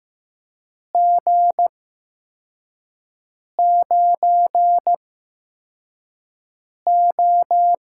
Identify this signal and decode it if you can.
Morse code, heard as a keyed tone, about 15 wpm, spaced as for 4 wpm, 700 Hz. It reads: G9O
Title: USA Call Signs - 4 Characters - 8x Character Spacing - 15wpm